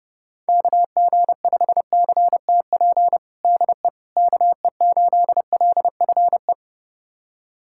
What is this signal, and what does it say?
Morse code, heard as a keyed tone, about 30 wpm, 700 Hz. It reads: KG5CTP DE KE8LFE
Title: USA Callsign DE Callsign - 6 Characters - 30wpm